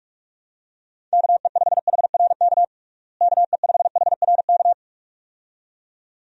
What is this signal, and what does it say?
Morse code, heard as a keyed tone, about 42 wpm, 700 Hz. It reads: KE5HRK KE5HRK
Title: USA Call Signs - 6 Characters - Repeated 2x - 45wpm